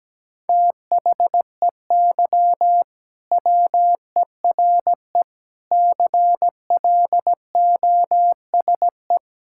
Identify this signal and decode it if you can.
Morse code, heard as a keyed tone, about 17 wpm, 700 Hz. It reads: THEY WERE CLOSE
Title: Sentences from Top 1000 Words - Review Part 6 - 17wpm